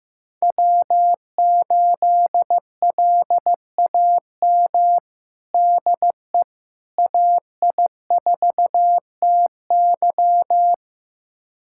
Morse code, heard as a keyed tone, about 15 wpm, 700 Hz. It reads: W8LAM DE AI4TY
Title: USA Callsign DE Callsign - 5 Characters - 15wpm